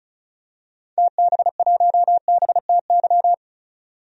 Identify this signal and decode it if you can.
Morse code, heard as a keyed tone, about 36 wpm, 700 Hz. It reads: TB1BTY
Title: DXCC-100 Call Signs - 6 Characters - 35wpm